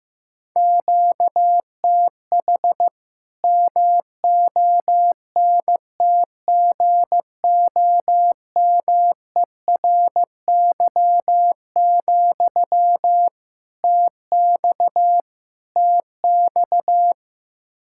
Morse code, heard as a keyed tone, about 15 wpm, 700 Hz. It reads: QTH MONTGOMERY, TX TX
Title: Practice QSO - Exchange 1 - Beginning - 15wpm